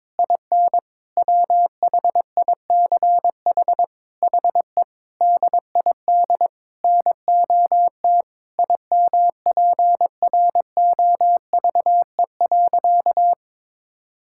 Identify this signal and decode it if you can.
Morse code, heard as a keyed tone, about 22 wpm, 700 Hz. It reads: IN WHICH HE DID NOT IMPROVE.
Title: Sentences from Top 1000 Words - Review Part 5 - 22wpm